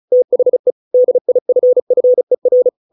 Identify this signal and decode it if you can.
Morse code, heard as a keyed tone, about 35 wpm, 500 Hz. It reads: THE DIFFER